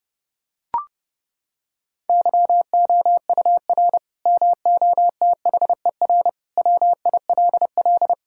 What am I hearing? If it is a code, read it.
Morse code, heard as a keyed tone, about 31 wpm, 700 Hz. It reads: YOUR MOTHER WILL
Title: Sentences from Top 1600 Words 30wpm